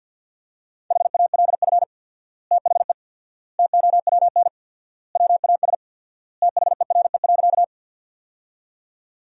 Morse code, heard as a keyed tone, about 53 wpm, 700 Hz. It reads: HALF THE TOWN WAS THERE.
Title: 4x Word-Spacing - Sentences from Top 500 Words - 50wpm